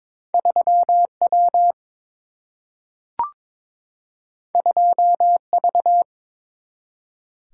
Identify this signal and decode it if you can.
Morse code, heard as a keyed tone, about 21 wpm, 700 Hz. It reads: 3 W 2 V
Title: Single-Letter Single-Number 22wpm